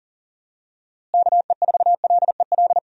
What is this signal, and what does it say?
Morse code, heard as a keyed tone, about 37 wpm, 700 Hz. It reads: KE4LEL
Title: USA Call Signs - 6 Characters - Repeated 3x - 40wpm